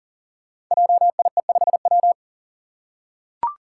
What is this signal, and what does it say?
Morse code, heard as a keyed tone, about 42 wpm, 700 Hz. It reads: JIE5W
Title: Code Groups - 5 Letter:Numbers - 40wpm